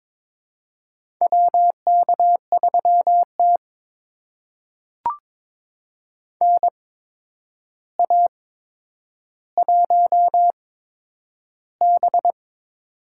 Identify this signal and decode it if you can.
Morse code, heard as a keyed tone, about 22 wpm, spaced as for 9 wpm, 700 Hz. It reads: WK3T NA1B
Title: USA Call Signs - 4 Characters - 8x Character Spacing - 22wpm